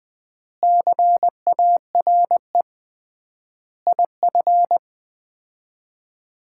Morse code, heard as a keyed tone, about 20 wpm, 700 Hz. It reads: CARE IF
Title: 3x Word-Spacing - Sets of 5 Words - 20wpm